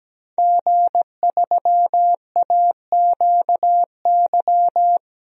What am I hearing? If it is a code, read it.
Morse code, heard as a keyed tone, about 17 wpm, 700 Hz. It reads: G3AQY
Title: DXCC-100 Call Signs - 5 Characters - 17wpm